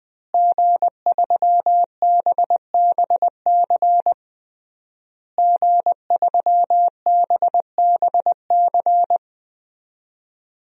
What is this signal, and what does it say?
Morse code, heard as a keyed tone, about 21 wpm, 700 Hz. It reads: G3BBC G3BBC
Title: DXCC-100 Call Signs - 5 Characters - Repeated 3x - 20wpm